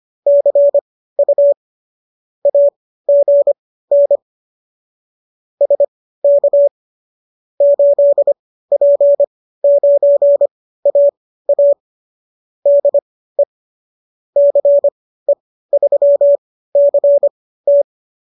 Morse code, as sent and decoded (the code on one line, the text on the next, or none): -.-. ..- / .- --. -. / ... -.- / ---.. .--. ----. .- .- / -.. / -.-. ...-- -.-. -
CU AGN SK 8P9AA DE CE3CT